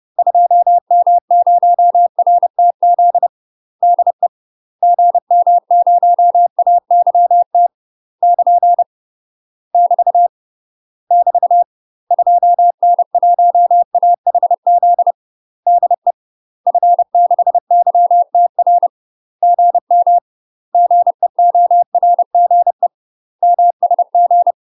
Morse code, heard as a keyed tone, about 30 wpm, 700 Hz. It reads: 2M0RTZ DE GM0AYT <KN> <BT> <BT> 2N1AHZ DE F6YTR GM GEORGE MSG
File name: UK MASTER QSOs TRAINING / 30WPM / UK MASTER QSOs 0028